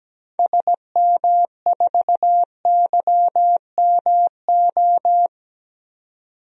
Morse code, heard as a keyed tone, about 17 wpm, 700 Hz. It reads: SM4YMO
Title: DXCC-100 Call Signs - 6 Characters - 17wpm